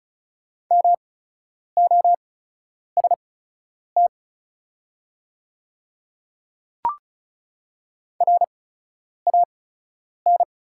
Morse code, heard as a keyed tone, about 35 wpm, spaced as for 10 wpm, 700 Hz. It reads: MOST RAN
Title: 8x Character Spacing - Top 500 Words - 35wpm